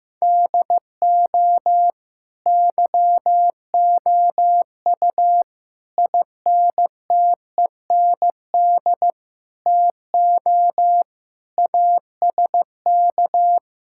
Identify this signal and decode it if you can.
Morse code, heard as a keyed tone, about 15 wpm, 700 Hz. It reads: DO YOU INTEND TO ASK
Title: Sentences from Top 1600 Words 15wpm